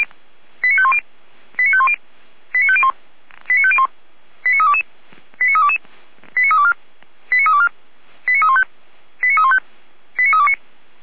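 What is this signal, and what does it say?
Five-tone selective call (ZVEI), heard as quick runs of five tones, tones of about 70 ms. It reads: 89510 89510 89501 89501 89322 89322 89324 89324 89315 89315 89329
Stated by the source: ZVEI five-tone call received by SDR.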